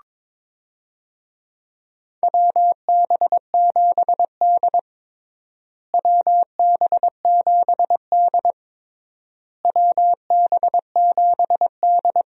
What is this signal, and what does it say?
Morse code, heard as a keyed tone, about 22 wpm, 700 Hz. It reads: WB7D WB7D WB7D
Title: USA Call Signs - 4 Characters - Repeated 3x - 22wpm